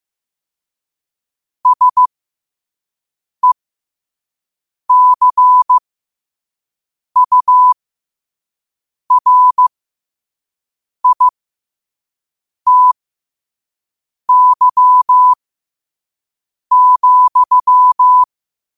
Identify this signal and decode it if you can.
Morse code, heard as a keyed tone, about 15 wpm, spaced as for 5 wpm, 1000 Hz. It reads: SECURITY,
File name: qotdfiles / 5wpm